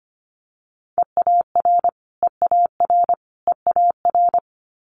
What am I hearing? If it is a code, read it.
Morse code, heard as a keyed tone, about 25 wpm, 700 Hz. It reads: EAR EAR EAR
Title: Multi-letter Phonemes Repeated 3x 25wpm